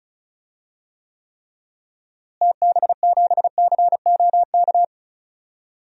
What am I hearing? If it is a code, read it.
Morse code, heard as a keyed tone, about 34 wpm, 700 Hz. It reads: TB7COK
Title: DXCC-100 Call Signs - 6 Characters - Repeated 3x - 35wpm